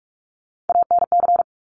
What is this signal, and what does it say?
Morse code, heard as a keyed tone, about 46 wpm, 700 Hz. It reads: ANC
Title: Top Three Letter Combinations in Words 45wpm